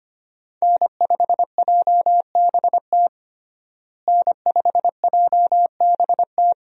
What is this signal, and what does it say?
Morse code, heard as a keyed tone, about 25 wpm, 700 Hz. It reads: N5JBT N5JBT
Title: USA Call Signs - 5 Characters - Repeated 3x - 25wpm